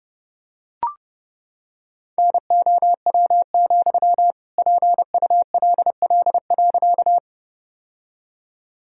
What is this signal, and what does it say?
Morse code, heard as a keyed tone, about 30 wpm, 700 Hz. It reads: NOW, PULL.
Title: Sentences from Top 1000 Words - Review Part 2 - 30wpm